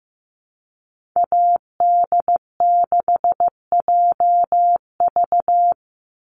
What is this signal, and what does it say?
Morse code, heard as a keyed tone, about 15 wpm, 700 Hz. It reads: AD6JV
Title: Call Signs Repeated 3x - 15wpm